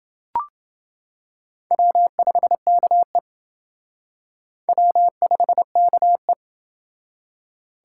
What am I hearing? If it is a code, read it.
Morse code, heard as a keyed tone, about 27 wpm, 700 Hz. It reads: W5KE W5KE